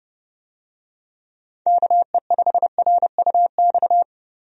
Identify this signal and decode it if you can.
Morse code, heard as a keyed tone, about 32 wpm, 700 Hz. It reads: KE5RUX
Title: USA Call Signs - 6 Characters - Repeated 2x - 30wpm